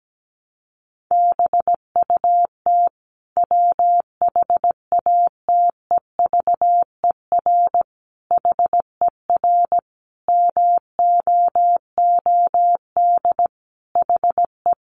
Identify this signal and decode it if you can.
Morse code, heard as a keyed tone, about 17 wpm, 700 Hz. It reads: BUT WHATEVER HER MOOD HE
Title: Sentences from Top 1100 Words 17wpm